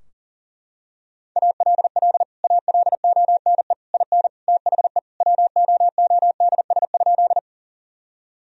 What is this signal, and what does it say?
Morse code, heard as a keyed tone, about 40 wpm, 700 Hz. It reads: ALL ALONE IN THE WOODS?
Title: Sentences from Top 1000 Words - Review Part 2 - 40wpm